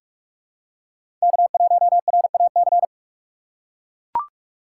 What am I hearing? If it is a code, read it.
Morse code, heard as a keyed tone, about 43 wpm, 700 Hz. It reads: K1RAC
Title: USA Call Signs - 5 Characters - 2x Character Spacing - 45wpm